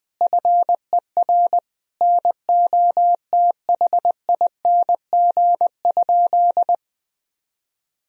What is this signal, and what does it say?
Morse code, heard as a keyed tone, about 20 wpm, 700 Hz. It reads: FER NOTHING?
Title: Sentences from Top 500 Words and Abbreviations 20wpm